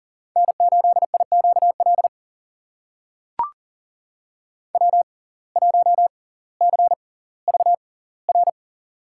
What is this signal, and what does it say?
Morse code, heard as a keyed tone, about 40 wpm, spaced as for 20 wpm, 700 Hz. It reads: N8IQL W1CVR
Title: USA Call Signs - 5 Characters - 6x Character Spacing - 40wpm